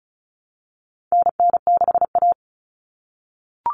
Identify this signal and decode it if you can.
Morse code, heard as a keyed tone, about 34 wpm, 700 Hz. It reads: NN6A
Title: Common Cut Numbers - 4-Digit 35wpm